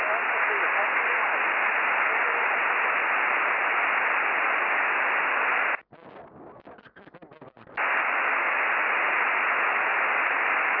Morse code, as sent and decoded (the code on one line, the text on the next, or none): none